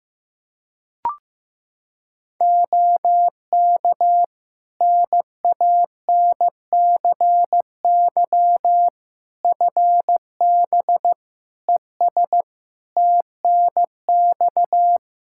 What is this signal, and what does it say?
Morse code, heard as a keyed tone, about 16 wpm, 700 Hz. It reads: OK NANCY FB ES TNX
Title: Practice QSO - Exchange 2 - Info - 15wpm